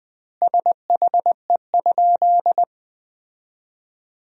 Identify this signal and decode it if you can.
Morse code, heard as a keyed tone, about 20 wpm, 700 Hz. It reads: SHE?
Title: 2x Word-Spacing - Sentences from Top 100 Words - 20wpm